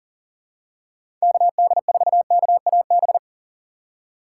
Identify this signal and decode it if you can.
Morse code, heard as a keyed tone, about 40 wpm, 700 Hz. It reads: KD4KAB
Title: USA Call Signs - 6 Characters - 4x Character Spacing - 40wpm